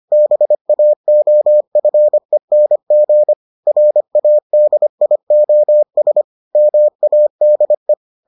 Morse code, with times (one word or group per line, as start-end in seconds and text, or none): 0.1-3.3 BAOFENG
3.7-6.2 RADIOS
6.6-7.9 MADE